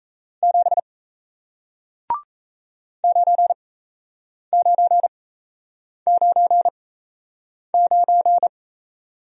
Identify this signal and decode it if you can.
Morse code, heard as a keyed tone, about 37 wpm, spaced as for 8 wpm, 700 Hz. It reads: 7 9999